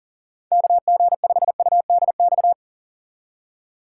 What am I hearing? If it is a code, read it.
Morse code, heard as a keyed tone, about 40 wpm, 700 Hz. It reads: KG5UDX
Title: USA Call Signs - 6 Characters - 6x Character Spacing - 40wpm